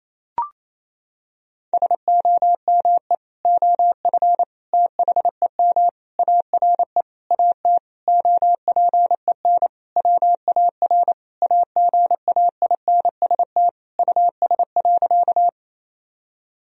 Morse code, heard as a keyed tone, about 28 wpm, 700 Hz. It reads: SOME OF THEM ARE AT OPEN WAR AGAINST US.
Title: Sentences from Top 1000 Words - Review Part 2 - 28wpm